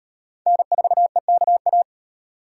Morse code, heard as a keyed tone, about 37 wpm, 700 Hz. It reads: N4EKA